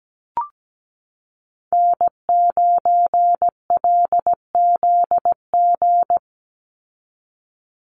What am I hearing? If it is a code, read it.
Morse code, heard as a keyed tone, about 17 wpm, 700 Hz. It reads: N9LZG
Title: USA Call Signs 17wpm